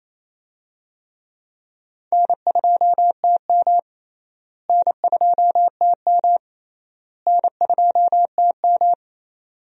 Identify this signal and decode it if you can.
Morse code, heard as a keyed tone, about 28 wpm, 700 Hz. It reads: N2TM N2TM N2TM
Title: USA Call Signs - 4 Characters - Repeated 3x - 28wpm